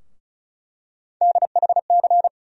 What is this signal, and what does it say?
Morse code, heard as a keyed tone, about 36 wpm, 700 Hz. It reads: DHC
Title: USA Call Sign Postfixes - 3 Letters 35wpm